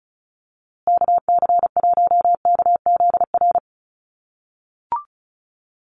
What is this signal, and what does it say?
Morse code, heard as a keyed tone, about 34 wpm, 700 Hz. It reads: KC1KZR
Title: USA Call Signs Repeated 2x 35wpm